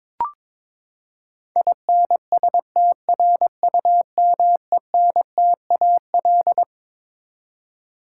Morse code, heard as a keyed tone, about 21 wpm, 700 Hz. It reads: INSTRUMENTAL
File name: Twelve Letter Words - 22wpm